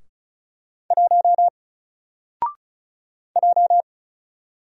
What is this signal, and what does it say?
Morse code, heard as a keyed tone, about 36 wpm, 700 Hz. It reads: 1 J
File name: Single-Letter-Number-35wpm